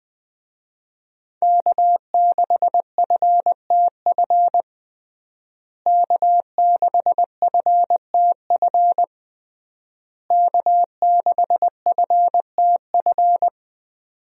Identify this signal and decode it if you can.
Morse code, heard as a keyed tone, about 19 wpm, 700 Hz. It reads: K6FTF K6FTF K6FTF
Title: USA Call Signs - 5 Characters - Repeated 3x - 20wpm